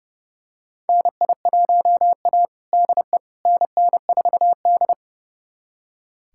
Morse code, heard as a keyed tone, about 30 wpm, 700 Hz. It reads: NI1A DE NN4D
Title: USA Callsign DE Callsign - 4 Characters - 30wpm